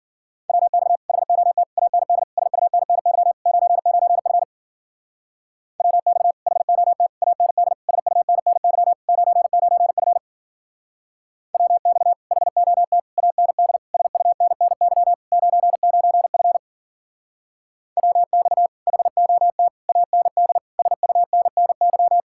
Morse code, heard as a keyed tone, about 52 wpm, 700 Hz. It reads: WX HOT AND SUNNY 99F WX HOT AND SUNNY 99F WX HOT AND SUNNY 99F WX HOT AND SUNNY